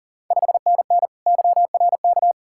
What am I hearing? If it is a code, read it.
Morse code, heard as a keyed tone, about 40 wpm, 690 Hz. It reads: 5NN YRK
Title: Ontario QSO Party Exchange 40wpm